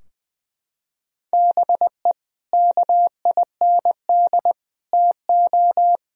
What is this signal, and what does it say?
Morse code, heard as a keyed tone, about 21 wpm, 700 Hz. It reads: BE KIND TO
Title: Sentences from Top 1000 Words - Review Part 2 - 20wpm